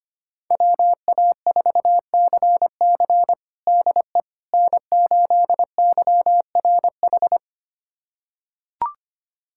Morse code, heard as a keyed tone, about 25 wpm, 700 Hz. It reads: WA4CC DE N8YRH
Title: USA Callsign DE Callsign - 5 Characters - 25wpm